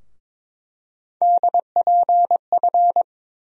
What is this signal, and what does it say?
Morse code, heard as a keyed tone, about 22 wpm, 700 Hz. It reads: DPF
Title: USA Call Sign Postfixes - 3 Letters 22wpm